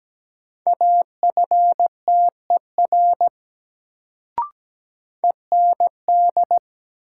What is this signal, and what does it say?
Morse code, heard as a keyed tone, about 17 wpm, 700 Hz. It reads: AFTER END
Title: Sentences from Top 200 Words - New Word Drill - 17wpm